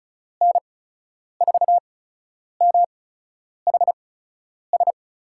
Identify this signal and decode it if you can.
Morse code, heard as a keyed tone, about 36 wpm, spaced as for 10 wpm, 700 Hz. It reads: N4MHS
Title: USA Call Signs - 6 Characters - 8x Character Spacing - 35wpm